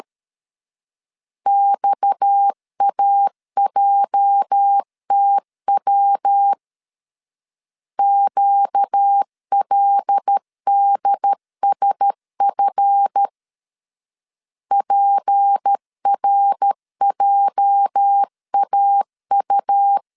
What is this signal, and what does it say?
Morse code, heard as a keyed tone, about 13 wpm, 800 Hz. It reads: XAJTW QLDSF PRJAU
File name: LEZIONI / 8 YJ45 / 8 YJ 12wpm 2235